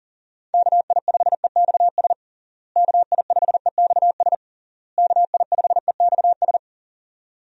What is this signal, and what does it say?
Morse code, heard as a keyed tone, about 40 wpm, 700 Hz. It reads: KI5EXS KI5EXS KI5EXS
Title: USA Call Signs - 6 Characters - Repeated 3x - 40wpm